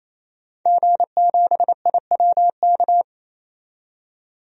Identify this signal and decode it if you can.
Morse code, heard as a keyed tone, about 28 wpm, 700 Hz. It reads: G7IWK
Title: DXCC-100 Call Signs - 5 Characters - 28wpm